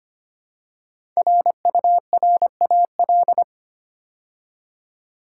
Morse code, heard as a keyed tone, about 25 wpm, 700 Hz. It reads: RURAL
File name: 5 Letter Words - First 3 Letters Given - 25wpm